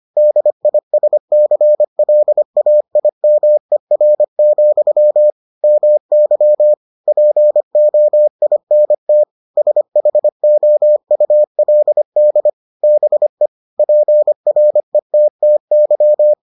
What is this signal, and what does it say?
Morse code, heard as a keyed tone, about 25 wpm, 600 Hz. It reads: DISCLAIMER, MY POINT SHOULD BE PRETTY